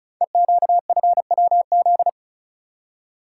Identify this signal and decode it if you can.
Morse code, heard as a keyed tone, about 36 wpm, 700 Hz. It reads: EQFWZ
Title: Code Groups - 5 Letter:Numbers - 35wpm